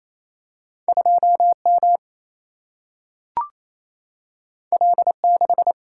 Morse code, heard as a keyed tone, about 28 wpm, 700 Hz. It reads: 2 M L 6
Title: Single-Letter Single-Number 28wpm